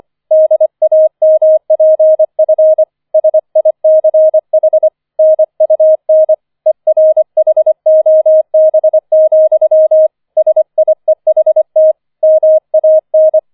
Morse code, heard as a keyed tone, about 25 wpm, 630 Hz. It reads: DAMPF SICH NUN ERHOB, SIEHT MAN
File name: 004 Vierter Streich 25wpm